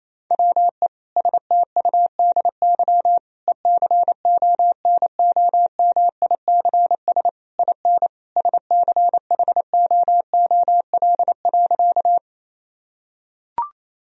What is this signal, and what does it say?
Morse code, heard as a keyed tone, about 28 wpm, 700 Hz. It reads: WE STUDY ECONOMICS IN SCHOOL.
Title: Sentences from Top 1000 Words - Review Part 3 - 28wpm